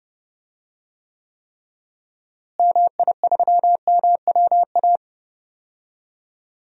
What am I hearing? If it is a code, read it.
Morse code, heard as a keyed tone, about 30 wpm, 700 Hz. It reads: MI3MWA
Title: DXCC-100 Call Signs - 6 Characters - Repeated 3x - 30wpm